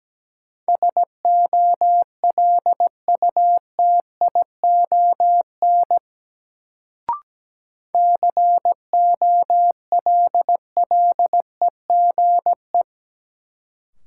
Morse code, heard as a keyed tone, about 17 wpm, 700 Hz. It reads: SOLUTION COLLEGE
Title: Sentences from Top 1300 Words - New Word Drill Alt - 17wpm